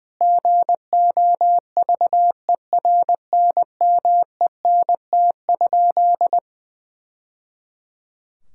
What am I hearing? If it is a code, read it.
Morse code, heard as a keyed tone, about 20 wpm, 700 Hz. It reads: GOVERNMENT?
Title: Sentences from Top 1000 Words - Review Part 5 - 20wpm